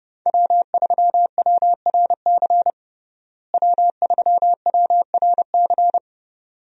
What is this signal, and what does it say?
Morse code, heard as a keyed tone, about 30 wpm, 700 Hz. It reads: W3WRC W3WRC
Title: USA Call Signs - 5 Characters - Repeated 2x - 30wpm